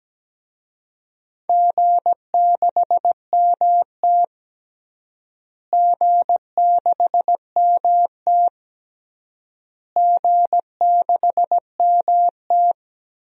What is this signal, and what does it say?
Morse code, heard as a keyed tone, about 17 wpm, 700 Hz. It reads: G6MT G6MT G6MT
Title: DXCC-100 Call Signs - 4 Characters - Repeated 3x - 17wpm